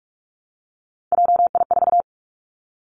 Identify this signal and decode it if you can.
Morse code, heard as a keyed tone, about 46 wpm, 700 Hz. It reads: JI4
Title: Three Challenging Letter-Number Combinations 45wpm